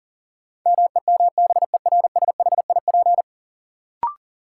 Morse code, heard as a keyed tone, about 40 wpm, 700 Hz. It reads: MEMBERSHIP
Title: Sentences from Top 1400 Words - New Word Drill Alt - 40wpm